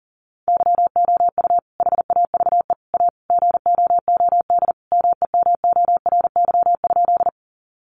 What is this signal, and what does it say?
Morse code, heard as a keyed tone, about 40 wpm, 700 Hz. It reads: YOU HAVE A GOOD MEMORY?
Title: Sentences from Top 1300 Words 40wpm